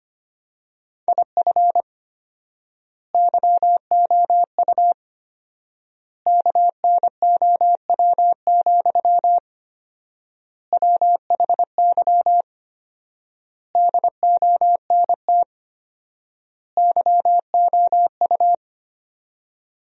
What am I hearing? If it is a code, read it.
Morse code, heard as a keyed tone, about 25 wpm, 700 Hz. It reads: IF YOU KNOW, WHY DONT YOU
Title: 4x Word-Spacing - Sentences from Top 300 Words - 25wpm